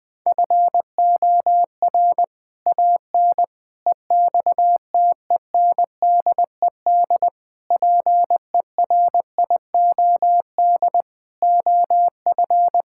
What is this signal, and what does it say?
Morse code, heard as a keyed tone, about 21 wpm, 700 Hz. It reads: FOR AN EXTENDED PERIOD OF
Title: Sets of 6 Words - 20wpm